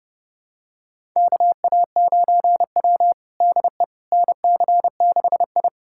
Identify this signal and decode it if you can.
Morse code, heard as a keyed tone, about 30 wpm, 700 Hz. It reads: KA9W DE NC6I
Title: USA Callsign DE Callsign - 4 Characters - 30wpm